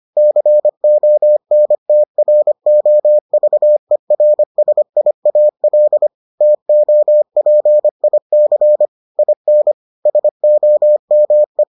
Morse code, heard as a keyed tone, about 25 wpm, 600 Hz. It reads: CONTROVERSIAL TOPIC IN SOME